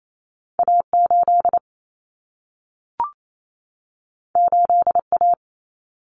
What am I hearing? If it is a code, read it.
Morse code, heard as a keyed tone, about 28 wpm, 700 Hz. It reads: A8 8A
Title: Common Cut Numbers - 2-Digit 28wpm